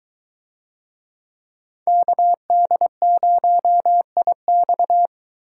Morse code, heard as a keyed tone, about 23 wpm, 700 Hz. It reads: KD0IX